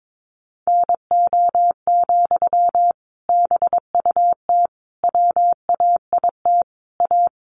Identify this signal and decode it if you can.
Morse code, heard as a keyed tone, about 22 wpm, 700 Hz. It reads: NO, BUT WAIT A
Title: Sentences from Top 900 Words 22wpm